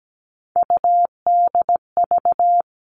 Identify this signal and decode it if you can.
Morse code, heard as a keyed tone, about 17 wpm, 700 Hz. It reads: UDV
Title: All Cut Numbers - 3-Digit 17wpm